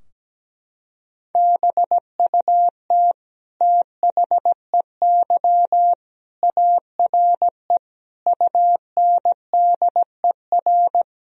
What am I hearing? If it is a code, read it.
Morse code, heard as a keyed tone, about 17 wpm, 700 Hz. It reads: BUT THEY ARE UNDER
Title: Sentences from Top 1000 Words - Review Part 4 - 17wpm